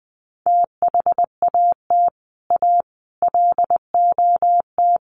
Morse code, heard as a keyed tone, about 20 wpm, 700 Hz. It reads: THAT A LOT
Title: Sets of 5 Words 20wpm